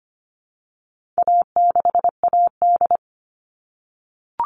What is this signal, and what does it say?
Morse code, heard as a keyed tone, about 25 wpm, 700 Hz. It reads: A6AD
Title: All Cut Numbers - 4-Digit 25wpm